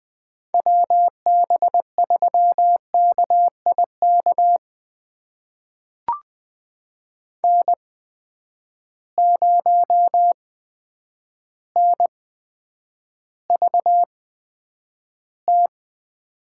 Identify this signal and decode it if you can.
Morse code, heard as a keyed tone, about 20 wpm, spaced as for 9 wpm, 700 Hz. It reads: WB3KIK N0NVT
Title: USA Call Signs - 8x Character Spacing - 20wpm